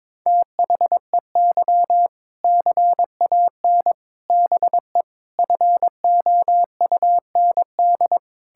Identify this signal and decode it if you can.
Morse code, heard as a keyed tone, about 22 wpm, 700 Hz. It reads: THEY CAN BE FOUND
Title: Sentences from Top 1600 Words 22wpm